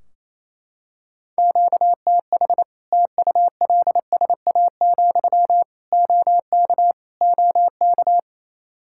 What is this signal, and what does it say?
Morse code, heard as a keyed tone, about 28 wpm, 700 Hz. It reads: QTH TULSA, OK OK
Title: Practice QSO - Exchange 1 - Beginning - 28wpm